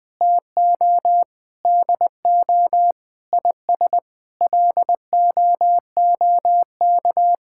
Morse code, heard as a keyed tone, about 21 wpm, 700 Hz. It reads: TO DO IS LOOK
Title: Sets of 9 Words - 20wpm